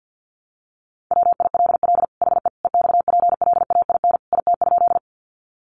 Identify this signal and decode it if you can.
Morse code, heard as a keyed tone, about 50 wpm, 700 Hz. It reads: WILL HE EXPLAIN IT?
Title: Sentences from Top 1100 Words 50wpm